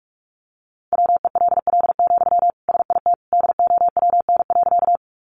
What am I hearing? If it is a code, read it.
Morse code, heard as a keyed tone, about 44 wpm, 700 Hz. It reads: WELL, SIT DOWN.
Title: Sentences from Top 500 Words 45wpm